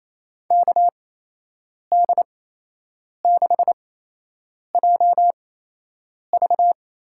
Morse code, heard as a keyed tone, about 28 wpm, spaced as for 8 wpm, 700 Hz. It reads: KD6JV
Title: USA Call Signs - 6 Characters - 8x Character Spacing - 28wpm